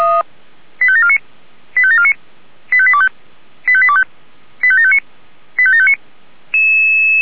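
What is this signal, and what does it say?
Five-tone selective call (ZVEI), heard as quick runs of five tones, tones of about 70 ms. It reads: 85749 85749 85825 85825 85759 85759
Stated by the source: ZVEI five-tone selective call received on an SDR.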